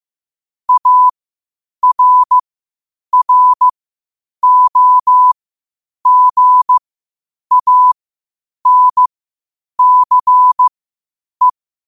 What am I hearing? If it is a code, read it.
Morse code, heard as a keyed tone, about 15 wpm, spaced as for 8 wpm, 1000 Hz. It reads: ARROGANCE
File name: qotdfiles / 8wpm